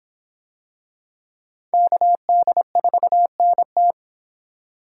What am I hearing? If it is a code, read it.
Morse code, heard as a keyed tone, about 23 wpm, 700 Hz. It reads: KD4NT